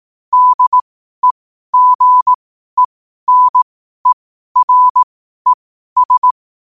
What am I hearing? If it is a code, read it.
Morse code, heard as a keyed tone, about 18 wpm, spaced as for 12 wpm, 1000 Hz. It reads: DEGENERES
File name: qotdfiles / 12wpm